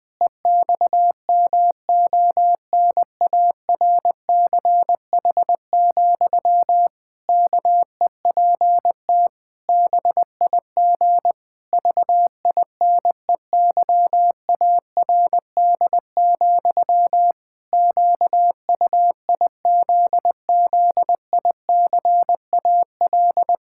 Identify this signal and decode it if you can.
Morse code, heard as a keyed tone, about 20 wpm, 700 Hz. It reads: EXMONARCH, KEPT BIG VINEYARD, QUIZZICAL
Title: Pangram Challenge v2 - 20wpm